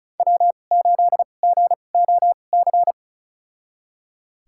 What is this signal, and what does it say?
Morse code, heard as a keyed tone, about 36 wpm, spaced as for 25 wpm, 700 Hz. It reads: W8GOC
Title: USA Call Signs - 5 Characters - 2x Character Spacing - 35wpm